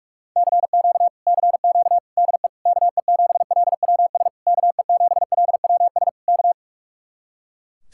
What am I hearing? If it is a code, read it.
Morse code, heard as a keyed tone, about 58 wpm, 700 Hz. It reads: CQ CQ DE KE7LWS KE7LWS K